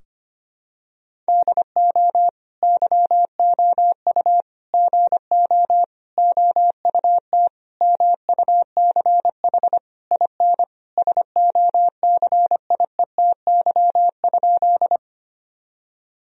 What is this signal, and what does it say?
Morse code, heard as a keyed tone, about 25 wpm, 700 Hz. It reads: DO YOU GO OUT MUCH IN SOCIETY?
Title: Sentences from Top 1000 Words - Review Part 4 - 25wpm